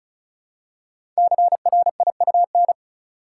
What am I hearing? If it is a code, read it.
Morse code, heard as a keyed tone, about 36 wpm, 700 Hz. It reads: CRIUN
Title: Code Groups - 5 Letters - 35wpm